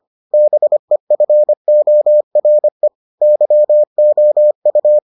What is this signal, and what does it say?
Morse code, heard as a keyed tone, about 25 wpm, 600 Hz. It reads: BEFORE YOU